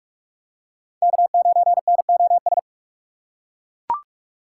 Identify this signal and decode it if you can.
Morse code, heard as a keyed tone, about 41 wpm, 700 Hz. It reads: K9NOS